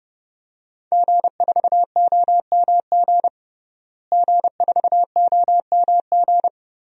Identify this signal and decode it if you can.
Morse code, heard as a keyed tone, about 32 wpm, 700 Hz. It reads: G4OMG G4OMG
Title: DXCC-100 Call Signs - 5 Characters - Repeated 3x - 30wpm